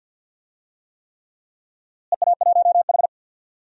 Morse code, heard as a keyed tone, about 46 wpm, 700 Hz. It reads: EA1H
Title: DXCC-100 Call Signs - 4 Characters - Repeated 3x - 50wpm